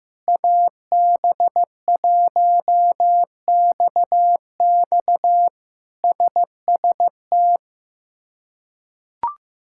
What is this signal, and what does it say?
Morse code, heard as a keyed tone, about 15 wpm, 700 Hz. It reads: AB1XX SST
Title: K1USN SST QSO Practice - 15wpm